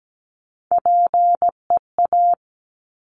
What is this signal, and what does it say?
Morse code, heard as a keyed tone, about 17 wpm, 700 Hz. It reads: PEA
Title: Top Three Letter Combinations in Words 17wpm